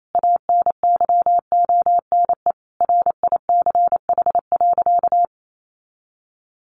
Morse code, heard as a keyed tone, about 28 wpm, 700 Hz. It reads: ANYONE RICH.
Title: Sentences from Top 1100 Words 28wpm